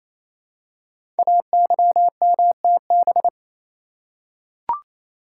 Morse code, heard as a keyed tone, about 27 wpm, 700 Hz. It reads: AYMTB
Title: Code Groups - 5 Letters - 28wpm